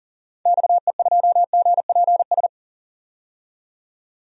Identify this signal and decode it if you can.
Morse code, heard as a keyed tone, about 39 wpm, 700 Hz. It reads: XE2GPS
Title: DXCC-100 Call Signs - 6 Characters - 40wpm